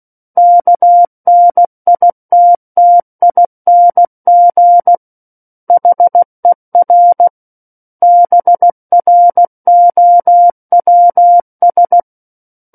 Morse code, heard as a keyed tone, about 16 wpm, 700 Hz. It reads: KNITTING HER BROWS
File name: Book One / 07-Chapter 7 - 16 WPM